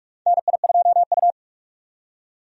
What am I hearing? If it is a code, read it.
Morse code, heard as a keyed tone, about 46 wpm, 700 Hz. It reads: NI2U
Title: USA Call Signs - 4 Characters - 45wpm